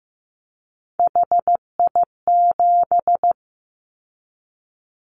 Morse code, heard as a keyed tone, about 15 wpm, 700 Hz. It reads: HI7
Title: Callsign Prefixes for Top 100 DXCC - 15wpm